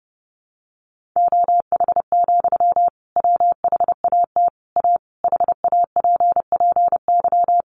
Morse code, heard as a keyed tone, about 30 wpm, 700 Hz. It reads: OH, WHAT A HAPPY
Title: Sentences from Top 1500 Words 30wpm